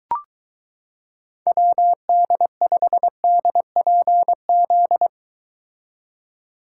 Morse code, heard as a keyed tone, about 23 wpm, 700 Hz. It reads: WD5DPZ